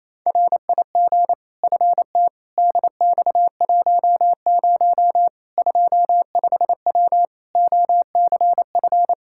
Morse code, heard as a keyed tone, about 27 wpm, 700 Hz. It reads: RIG FT DX10 25W OCF
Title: Practice QSO - Exchange 2 - Info - 28wpm